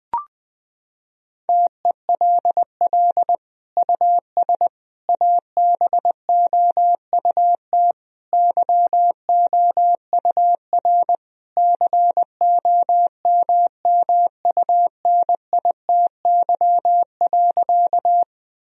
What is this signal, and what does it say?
Morse code, heard as a keyed tone, about 20 wpm, 700 Hz. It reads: TELL US ABOUT YOUR COMMUNITY.
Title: Sentences from Top 1000 Words - Review Part 4 - 20wpm